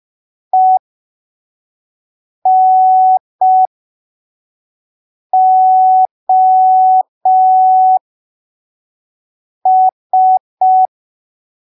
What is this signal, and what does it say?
Morse code, heard as a keyed tone, about 5 wpm, spaced as for 3 wpm, 750 Hz. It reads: ENOS
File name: arquivos / texto / texto 05 5ppm